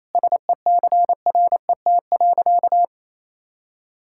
Morse code, heard as a keyed tone, about 28 wpm, 700 Hz. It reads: SECRET.
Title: Sentences from Top 1600 Words 28wpm